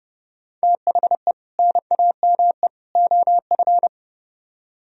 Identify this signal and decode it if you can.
Morse code, heard as a keyed tone, about 28 wpm, 700 Hz. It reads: THE NAME OF